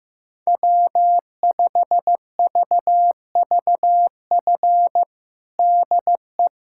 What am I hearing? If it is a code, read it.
Morse code, heard as a keyed tone, about 15 wpm, 700 Hz. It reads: W5VVF DE
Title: USA Callsign DE Callsign - 5 Characters - 15wpm